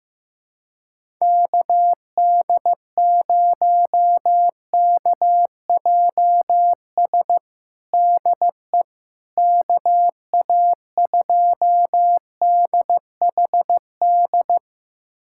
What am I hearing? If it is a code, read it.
Morse code, heard as a keyed tone, about 16 wpm, 700 Hz. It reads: KD0KJS DE KA2DHD
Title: USA Callsign DE Callsign - 6 Characters - 15wpm